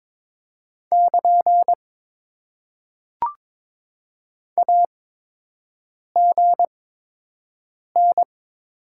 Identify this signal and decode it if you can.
Morse code, heard as a keyed tone, about 22 wpm, spaced as for 6 wpm, 700 Hz. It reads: <KN> AGN
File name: Top 90 QSO Elements - 8x Character Spacing - 22wpm